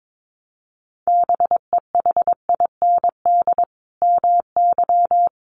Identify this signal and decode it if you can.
Morse code, heard as a keyed tone, about 22 wpm, 700 Hz. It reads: BEHIND MY
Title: Sentences from Top 500 Words 22wpm